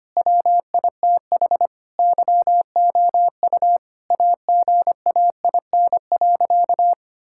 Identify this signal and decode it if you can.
Morse code, heard as a keyed tone, about 25 wpm, 700 Hz. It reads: WITH YOU AGAIN.
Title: Sentences from Top 1000 Words - Review Part 6 - 25wpm